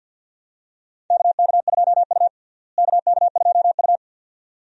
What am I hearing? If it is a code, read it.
Morse code, heard as a keyed tone, about 49 wpm, 700 Hz. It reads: KK2U KK2U
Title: USA Call Signs - 4 Characters - Repeated 2x - 50wpm